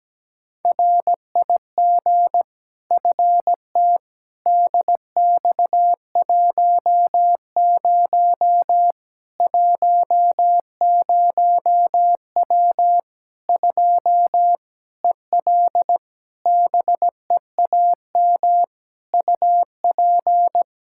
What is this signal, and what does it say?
Morse code, heard as a keyed tone, about 17 wpm, 700 Hz. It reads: RIG FT DX10 10W 2 EL BEAM UP
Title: Practice QSO - Exchange 2 - Info - 17wpm